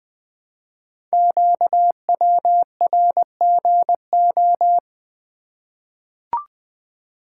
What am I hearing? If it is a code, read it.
Morse code, heard as a keyed tone, about 20 wpm, 700 Hz. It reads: QWRGO
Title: Code Groups - 5 Letters - 20wpm